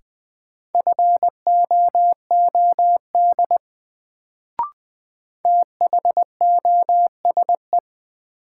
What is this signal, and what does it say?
Morse code, heard as a keyed tone, about 20 wpm, 700 Hz. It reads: FOOD THOSE
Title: Sentences from Top 300 Words - New Word Drill - 20wpm